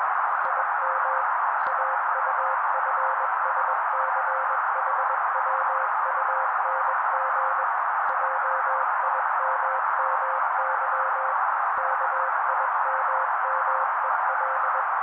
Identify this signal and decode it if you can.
Morse code, heard as a keyed tone, about 21 wpm, 530 Hz. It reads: IM AUFSCHWUNG JIMMY KIMMEL